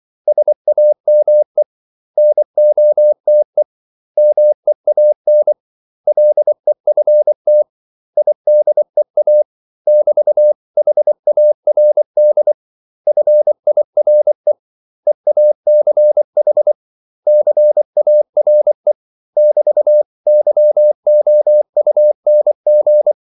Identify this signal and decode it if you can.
Morse code, heard as a keyed tone, about 24 wpm, 600 Hz. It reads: SAME NOTE MEAN LEFT IDEA <BT>HARD FIRE EACH CARE <BT>YOUNG
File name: CW / Practice / 500MostCommonEnglishWords24